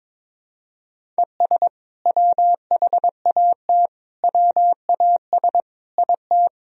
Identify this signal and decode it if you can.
Morse code, heard as a keyed tone, about 22 wpm, 700 Hz. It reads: ES WHAT WAS IT
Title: Sentences from Top 100 Words and Abbreviations 22wpm